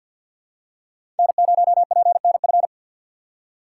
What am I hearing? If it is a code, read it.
Morse code, heard as a keyed tone, about 51 wpm, 700 Hz. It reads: N0PNF
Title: USA Call Signs - 5 Characters - 2x Character Spacing - 50wpm